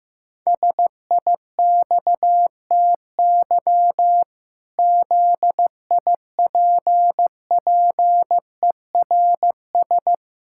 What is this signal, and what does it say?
Morse code, heard as a keyed tone, about 15 wpm, 700 Hz. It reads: SIXTY ZIPPERS
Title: Pangram Challenge v2 - 15wpm